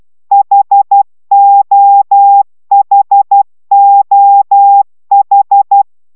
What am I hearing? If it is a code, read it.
Morse code, heard as a keyed tone, about 12 wpm, 800 Hz. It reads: HOHOH